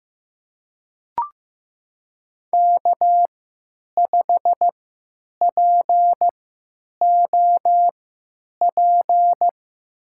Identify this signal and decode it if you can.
Morse code, heard as a keyed tone, about 15 wpm, spaced as for 9 wpm, 700 Hz. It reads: K5POP
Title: USA Call Signs - 5 Characters - 3x Character Spacing - 15wpm